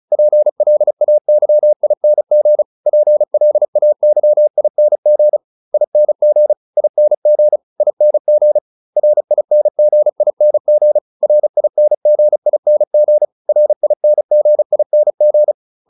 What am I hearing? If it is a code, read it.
Morse code, heard as a keyed tone, about 35 wpm, 600 Hz. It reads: PLAYING PLAYING ING ING ING RINGING RINGING RINGING
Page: advanced-practice-files